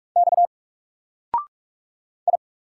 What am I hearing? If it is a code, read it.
Morse code, heard as a keyed tone, about 46 wpm, 700 Hz. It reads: X I
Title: Single Letter-Number 45wpm